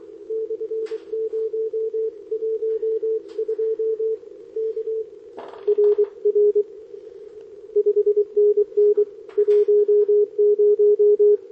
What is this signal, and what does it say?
Morse code, heard as a keyed tone, about 24 wpm, 410 Hz. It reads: /012 K RR 5NN 10
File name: OK QRP 2026 / OKQRP2026